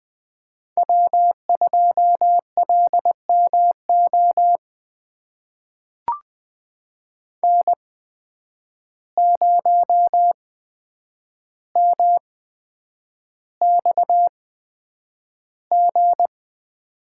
Morse code, heard as a keyed tone, about 20 wpm, spaced as for 9 wpm, 700 Hz. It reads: W2LMO N0MXG
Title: USA Call Signs - 5 Characters - 8x Character Spacing - 20wpm